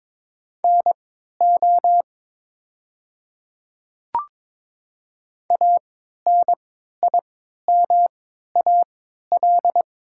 Morse code, heard as a keyed tone, about 23 wpm, spaced as for 13 wpm, 700 Hz. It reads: NO ANIMAL
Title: 3x Character Spacing - Top 200 Words - 22wpm